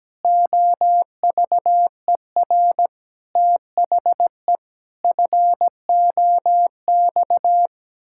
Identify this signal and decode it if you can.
Morse code, heard as a keyed tone, about 17 wpm, 700 Hz. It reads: OVER THE FOX
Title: Pangram Challenge v2 - 17wpm